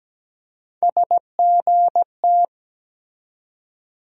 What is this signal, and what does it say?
Morse code, heard as a keyed tone, about 17 wpm, 700 Hz. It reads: SGT
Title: USA Call Sign Postfixes - 3 Letters 17wpm